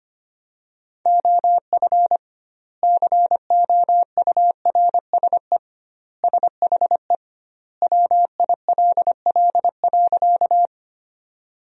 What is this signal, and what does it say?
Morse code, heard as a keyed tone, about 25 wpm, 700 Hz. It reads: OF COURSE SHE WILL.
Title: 2x Word-Spacing - Sentences from Top 500 Words - 25wpm